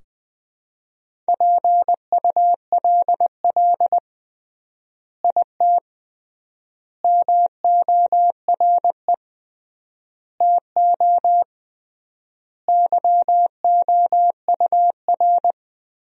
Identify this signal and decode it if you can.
Morse code, heard as a keyed tone, about 19 wpm, 700 Hz. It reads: PULL IT MORE TO YOUR
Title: 3x Word-Spacing - Sentences from Top 400 Words - 20wpm